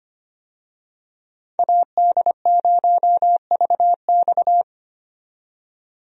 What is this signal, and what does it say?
Morse code, heard as a keyed tone, about 25 wpm, 690 Hz. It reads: AD0VX
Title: USA Call Signs - 5 Characters - 25wpm